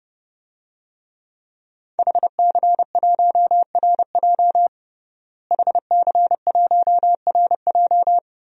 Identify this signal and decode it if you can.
Morse code, heard as a keyed tone, about 32 wpm, 700 Hz. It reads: HC1RJ HC1RJ
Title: DXCC-100 Call Signs - 5 Characters - Repeated 3x - 30wpm